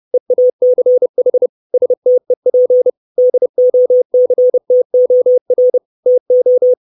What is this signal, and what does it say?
Morse code, heard as a keyed tone, about 30 wpm, 500 Hz. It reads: EACH STEP DOCTOR TO
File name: 632 Words 30WPM